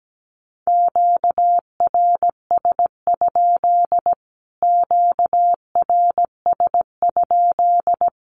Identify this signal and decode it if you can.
Morse code, heard as a keyed tone, about 17 wpm, 700 Hz. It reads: QRS? QRS?
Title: Q-Codes Repeated 3x 17wpm